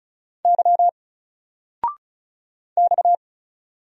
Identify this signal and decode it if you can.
Morse code, heard as a keyed tone, about 35 wpm, 700 Hz. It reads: Y X
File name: Single-Letter-Number-35wpm